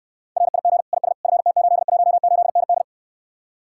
Morse code, heard as a keyed tone, about 64 wpm, 700 Hz. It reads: RIG IS FT818ND